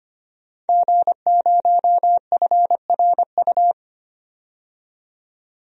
Morse code, heard as a keyed tone, about 25 wpm, 700 Hz. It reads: G0FRU
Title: DXCC-100 Call Signs - 5 Characters - 25wpm